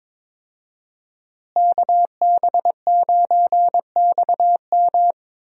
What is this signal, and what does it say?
Morse code, heard as a keyed tone, about 21 wpm, 700 Hz. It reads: KB9XM
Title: USA Call Signs - 5 Characters - Repeated 3x - 22wpm